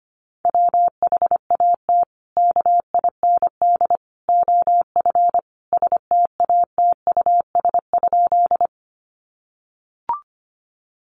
Morse code, heard as a keyed tone, about 25 wpm, 700 Hz. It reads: WHAT KIND OF STATUS?
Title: Sentences from Top 1400 Words 25wpm